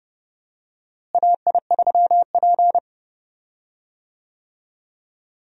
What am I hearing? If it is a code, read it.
Morse code, heard as a keyed tone, about 30 wpm, 700 Hz. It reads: AI3P
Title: USA Call Signs - 4 Characters - Repeated 3x - 30wpm